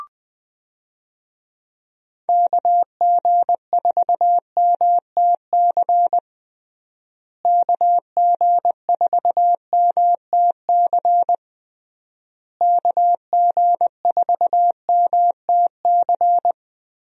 Morse code, heard as a keyed tone, about 20 wpm, 700 Hz. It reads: KG4MTC KG4MTC KG4MTC
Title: USA Call Signs - 6 Characters - Repeated 3x - 20wpm